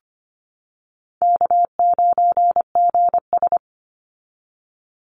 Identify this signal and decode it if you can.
Morse code, heard as a keyed tone, about 25 wpm, 700 Hz. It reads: K9GS
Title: Call Signs - 25wpm